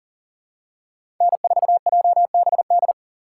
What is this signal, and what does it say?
Morse code, heard as a keyed tone, about 40 wpm, 700 Hz. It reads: N4JBD
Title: USA Call Signs - 5 Characters - 2x Character Spacing - 40wpm